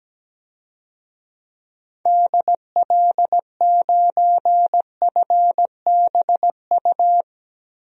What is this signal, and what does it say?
Morse code, heard as a keyed tone, about 17 wpm, 700 Hz. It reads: DL9FBU
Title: DXCC-100 Call Signs - 6 Characters - Repeated 3x - 17wpm